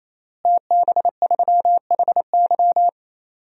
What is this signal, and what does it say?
Morse code, heard as a keyed tone, about 28 wpm, 700 Hz. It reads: TB3HY
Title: DXCC-100 Call Signs - 5 Characters - Repeated 3x - 28wpm